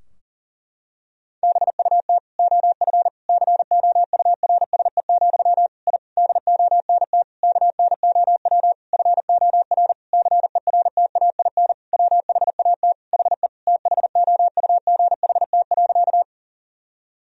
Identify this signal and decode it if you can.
Morse code, heard as a keyed tone, about 40 wpm, 700 Hz. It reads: BUT OF COURSE, I DONT KNOW FOR CERTAIN WHAT HE THOUGHT.
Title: Sentences from Top 1000 Words - Review Part 2 - 40wpm